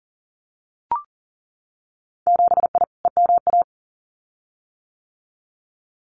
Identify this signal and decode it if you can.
Morse code, heard as a keyed tone, about 40 wpm, 700 Hz. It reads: 7I EMA
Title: Winter Field Day Exchanges Encore 40wpm